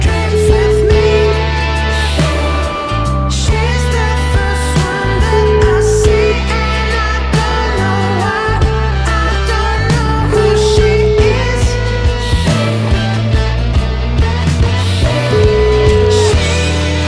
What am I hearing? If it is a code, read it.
Ringback tone (Europe, 425 Hz), with noise.